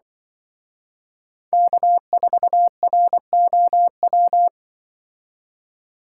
Morse code, heard as a keyed tone, about 24 wpm, 700 Hz. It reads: K4ROW
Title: Speed Racing - USA 1x3 Call Signs 20wpm